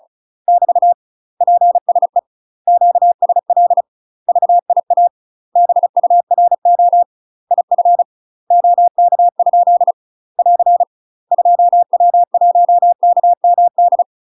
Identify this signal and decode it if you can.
Morse code, heard as a keyed tone, about 36 wpm, 700 Hz. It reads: <BT> PSE QSL VIA BURO IF OK? <AR> 2W1KMD
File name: UK MASTER QSOs TRAINING / 35WPM / UK MASTER QSOs 0033